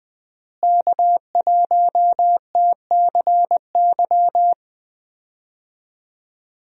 Morse code, heard as a keyed tone, about 21 wpm, 700 Hz. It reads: K1TCY
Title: USA Call Signs - 5 Characters - Repeated 3x - 20wpm